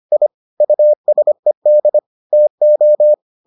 Morse code, heard as a keyed tone, about 25 wpm, 600 Hz. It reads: I USED TO